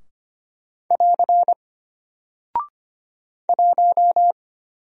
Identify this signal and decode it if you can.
Morse code, heard as a keyed tone, about 25 wpm, 700 Hz. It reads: <AR> 1
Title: Single Character (Letter:Number:Punctuation:Prosign) Alt 25wpm